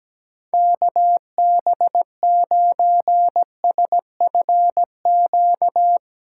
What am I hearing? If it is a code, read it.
Morse code, heard as a keyed tone, about 17 wpm, 700 Hz. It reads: KB9SFQ
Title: USA Callsign DE Callsign - 6 Characters - 17wpm